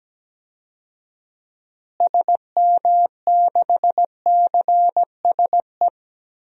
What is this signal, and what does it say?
Morse code, heard as a keyed tone, about 17 wpm, 700 Hz. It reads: SM6CSE
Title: DXCC-100 Call Signs - 6 Characters - Repeated 2x - 17wpm